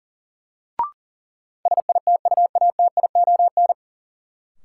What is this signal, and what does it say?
Morse code, heard as a keyed tone, about 40 wpm, 700 Hz. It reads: SITUATION
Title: Sentences from Top 1200 Words - New Word Drill - 40wpm